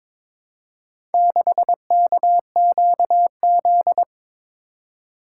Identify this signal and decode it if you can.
Morse code, heard as a keyed tone, about 21 wpm, 700 Hz. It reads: 6KQZ
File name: practice / 22wpm / Four Letter-Number Combinations - 22wpm